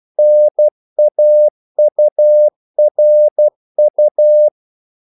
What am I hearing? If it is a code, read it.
Morse code, heard as a keyed tone, about 12 wpm, 600 Hz. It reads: NAURU
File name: CW / Practice / NationalCapitals12